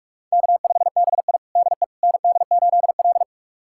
Morse code, heard as a keyed tone, about 41 wpm, 700 Hz. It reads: K5BI DE ND8L
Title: USA Callsign DE Callsign - 4 Characters - 45wpm